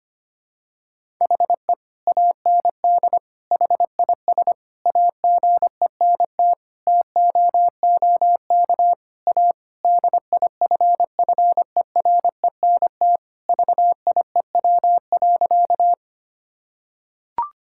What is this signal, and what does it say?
Morse code, heard as a keyed tone, about 25 wpm, 700 Hz. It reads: HE AND HIS AGENT TOOK A DIFFERENT VIEW.
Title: Sentences from Top 1600 Words 25wpm